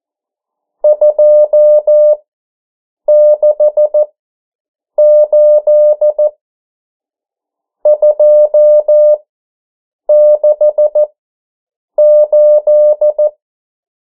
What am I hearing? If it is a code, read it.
Morse code, heard as a keyed tone, about 14 wpm, spaced as for 7 wpm, 600 Hz. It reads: 268 268
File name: morse mp3 / 09 urok 8